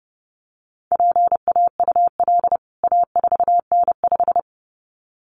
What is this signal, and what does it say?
Morse code, heard as a keyed tone, about 30 wpm, 700 Hz. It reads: PAUL A4N5
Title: CWT Exchange 30wpm